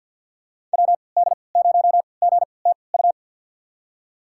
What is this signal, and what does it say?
Morse code, heard as a keyed tone, about 49 wpm, spaced as for 28 wpm, 700 Hz. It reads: WD0GTU
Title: USA Call Signs - 6 Characters - 3x Character Spacing - 50wpm